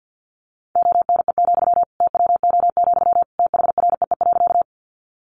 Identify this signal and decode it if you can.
Morse code, heard as a keyed tone, about 50 wpm, 700 Hz. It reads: ONE, TWO, THREE.
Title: Sentences from Top 200 Words 50wpm